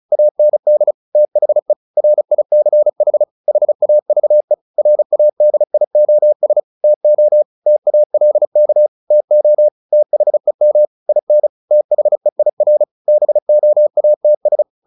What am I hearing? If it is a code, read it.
Morse code, heard as a keyed tone, about 34 wpm, 600 Hz. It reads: AND THE RICH HAVE RADIOS TO TALK TO THEM IN THEIR BOATS
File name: the old man and the sea / the old man and the sea 1 35wpm